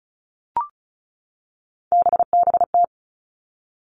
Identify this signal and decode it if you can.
Morse code, heard as a keyed tone, about 36 wpm, 700 Hz. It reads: BBT
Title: Three Challenging Letter-Number Combinations 35wpm